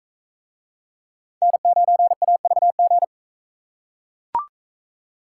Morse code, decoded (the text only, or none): N9AVG